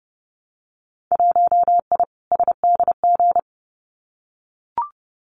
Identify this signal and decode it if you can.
Morse code, heard as a keyed tone, about 30 wpm, 700 Hz. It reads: 1I SDG
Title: Winter Field Day Exchanges 30wpm